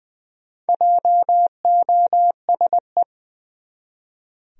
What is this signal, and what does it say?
Morse code, heard as a keyed tone, about 21 wpm, 700 Hz. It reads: JOSE
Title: USA Common Names - Max 5 Letters - 20wpm